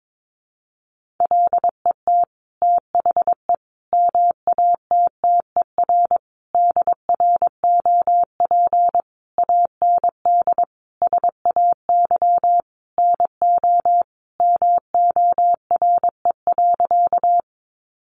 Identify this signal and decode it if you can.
Morse code, heard as a keyed tone, about 22 wpm, 700 Hz. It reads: LET THE MATTER DROP AND SAY NO MORE.
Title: Sentences from Top 1400 Words 22wpm